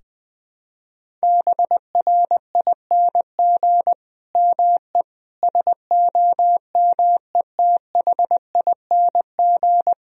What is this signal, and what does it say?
Morse code, heard as a keyed tone, about 20 wpm, 700 Hz. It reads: BRING ME SOMETHING
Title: Sentences from Top 1000 Words - Review Part 3 - 20wpm